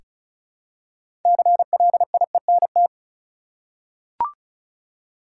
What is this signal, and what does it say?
Morse code, heard as a keyed tone, about 36 wpm, 700 Hz. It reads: CLIENT
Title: Six Letter Words - 35wpm